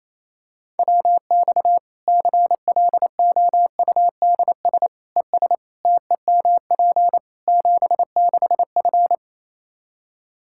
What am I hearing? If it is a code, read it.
Morse code, heard as a keyed tone, about 28 wpm, 700 Hz. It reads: WX CLOUDS ES TEMP 76F
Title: K1USN SST QSO Practice 28wpm